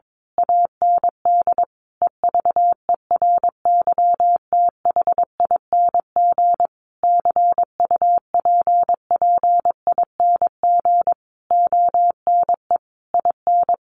Morse code, heard as a keyed tone, about 22 wpm, 700 Hz. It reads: AND EVERYTHING CUPPING ONE IN
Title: A Sound Of Thunder 22wpm